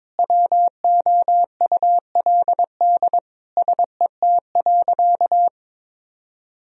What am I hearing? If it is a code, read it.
Morse code, heard as a keyed tone, about 22 wpm, 700 Hz. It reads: WOULD SET.
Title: Sentences from Top 1000 Words - Review Part 1 - 22wpm